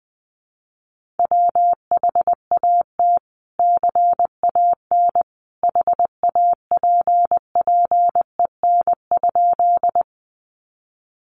Morse code, heard as a keyed tone, about 20 wpm, 700 Hz. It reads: WHAT CAN HAPPEN?
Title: Sentences from Top 400 Words 20wpm